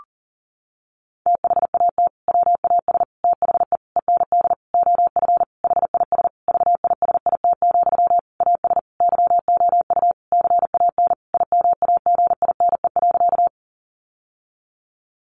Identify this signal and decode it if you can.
Morse code, heard as a keyed tone, about 40 wpm, 700 Hz. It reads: THAT WAS THE END OF HIS VISIT, AS YOU CAN IMAGINE.
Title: Sentences from Top 1400 Words 40wpm